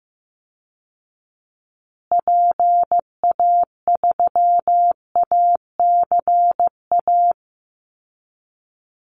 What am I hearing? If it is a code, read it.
Morse code, heard as a keyed tone, about 15 wpm, 700 Hz. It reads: PA3ACA
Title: Call Signs Repeated 3x - 15wpm